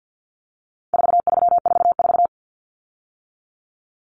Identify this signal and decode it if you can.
Morse code, heard as a keyed tone, about 50 wpm, 700 Hz. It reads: 4344
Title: Common Cut Numbers - 4-Digit 50wpm